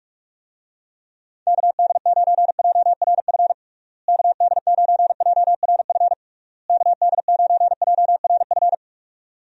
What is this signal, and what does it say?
Morse code, heard as a keyed tone, about 43 wpm, 700 Hz. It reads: KD9JRF KD9JRF KD9JRF
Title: USA Call Signs - 6 Characters - Repeated 3x - 45wpm